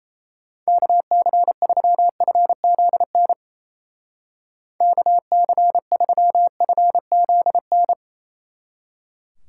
Morse code, heard as a keyed tone, about 34 wpm, 700 Hz. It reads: KC3FZN KC3FZN
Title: Speed Racing - USA 2x3 Call Signs 28wpm